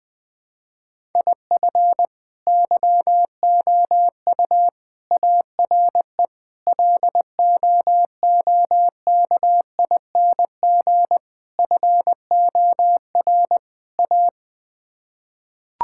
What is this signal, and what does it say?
Morse code, heard as a keyed tone, about 21 wpm, 700 Hz. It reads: IF YOU ARE LOOKING FOR A
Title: Sets of 6 Words - 20wpm